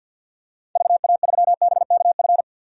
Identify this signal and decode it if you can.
Morse code, heard as a keyed tone, about 51 wpm, 700 Hz. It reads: VA3BKF
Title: DXCC-100 Call Signs - 6 Characters - Repeated 3x - 50wpm